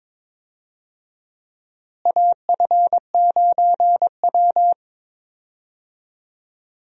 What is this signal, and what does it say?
Morse code, heard as a keyed tone, about 22 wpm, 700 Hz. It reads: AF9W
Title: USA Call Signs - 4 Characters - Repeated 2x - 22wpm